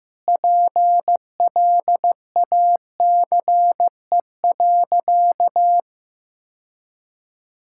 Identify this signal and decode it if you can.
Morse code, heard as a keyed tone, about 15 wpm, 700 Hz. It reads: PLACE.
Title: Sentences from Top 1600 Words 15wpm